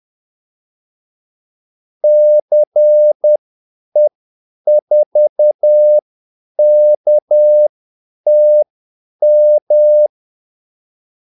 Morse code, heard as a keyed tone, about 10 wpm, spaced as for 8 wpm, 600 Hz. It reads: CE4KTM
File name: Callsigns-10-8wpm